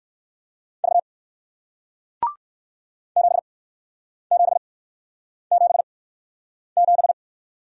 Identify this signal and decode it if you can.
Morse code, heard as a keyed tone, about 56 wpm, spaced as for 9 wpm, 700 Hz. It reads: V 7777
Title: Speed-Racing Single Letter-Number 45wpm